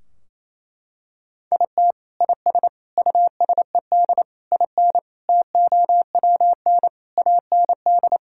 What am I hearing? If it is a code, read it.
Morse code, heard as a keyed tone, about 28 wpm, 700 Hz. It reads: IT IS USED IN TOWN AND
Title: Sentences from Top 1600 Words 28wpm